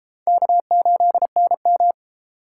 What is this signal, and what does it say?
Morse code, heard as a keyed tone, about 27 wpm, 700 Hz. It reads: K8NM
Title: Speed Racing - USA 1x2 Call Signs 22wpm